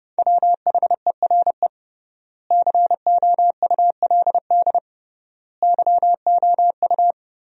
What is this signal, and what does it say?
Morse code, heard as a keyed tone, about 32 wpm, 700 Hz. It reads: WHERE COULD YOU
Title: 3x Word-Spacing - Sentences from Top 300 Words - 30wpm